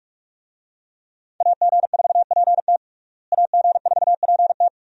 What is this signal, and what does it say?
Morse code, heard as a keyed tone, about 44 wpm, 700 Hz. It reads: AG4PT AG4PT
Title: USA Call Signs - 5 Characters - Repeated 2x - 45wpm